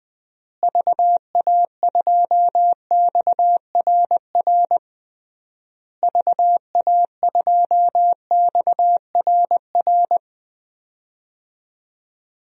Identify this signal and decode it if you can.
Morse code, heard as a keyed tone, about 21 wpm, 700 Hz. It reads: VA2XRR VA2XRR
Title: DXCC-100 Call Signs - 6 Characters - Repeated 3x - 20wpm